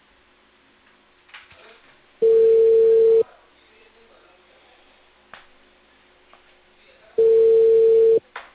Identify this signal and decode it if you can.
Ringback tone (China, 450 Hz).